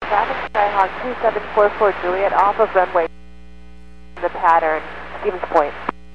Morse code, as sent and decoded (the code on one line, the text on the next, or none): none